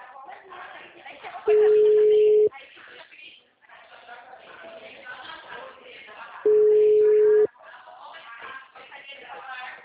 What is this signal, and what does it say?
Ringback tone (Europe, 425 Hz), with speech.
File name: out- -2010- - - .46015